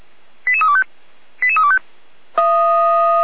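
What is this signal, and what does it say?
Five-tone selective call (ZVEI), heard as quick runs of five tones, tones of about 70 ms. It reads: 88325 88325
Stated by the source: ZVEI five-tone selective call received by SDR.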